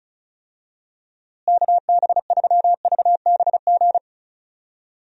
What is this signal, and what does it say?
Morse code, heard as a keyed tone, about 36 wpm, 700 Hz. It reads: KB3VBG
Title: USA Call Signs - 6 Characters - Repeated 3x - 35wpm